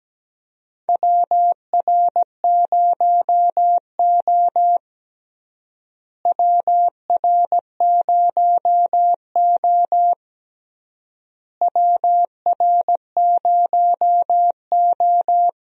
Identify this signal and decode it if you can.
Morse code, heard as a keyed tone, about 17 wpm, 700 Hz. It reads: WR0O WR0O WR0O
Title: USA Call Signs - 4 Characters - Repeated 3x - 17wpm